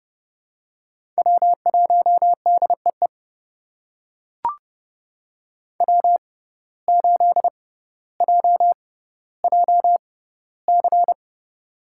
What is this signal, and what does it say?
Morse code, heard as a keyed tone, about 30 wpm, spaced as for 15 wpm, 700 Hz. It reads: W1DEE W8JJC
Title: USA Call Signs - 5 Characters - 6x Character Spacing - 30wpm